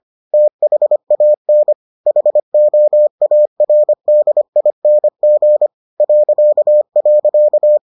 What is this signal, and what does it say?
Morse code, heard as a keyed tone, about 25 wpm, 600 Hz. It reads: THAN HOARDING ..